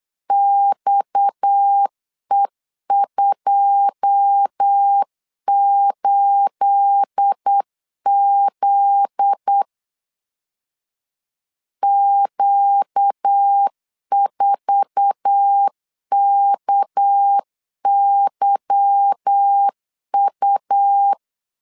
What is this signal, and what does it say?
Morse code, heard as a keyed tone, about 8 wpm, 800 Hz. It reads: XE28Z Q4KYU
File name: LEZIONI / 9 28 8wpm 3512